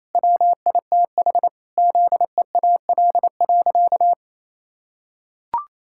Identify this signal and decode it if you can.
Morse code, heard as a keyed tone, about 28 wpm, 700 Hz. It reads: WITH ZEAL.
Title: Pangram Challenge v2 - 28wpm